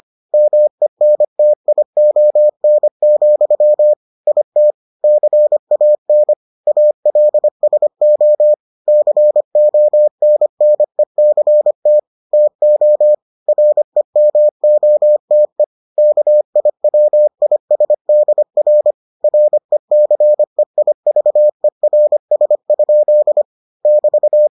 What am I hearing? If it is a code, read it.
Morse code, heard as a keyed tone, about 25 wpm, 600 Hz. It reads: MENTION, IT CAN ALSO CONNECT TO REMOTE KIWISDR RECEIVERS? <BT>